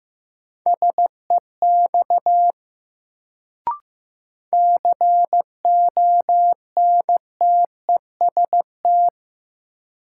Morse code, heard as a keyed tone, about 15 wpm, 700 Hz. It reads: SEX CONTEST
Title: Sentences from Top 1600 Words - New Word Drill - 15wpm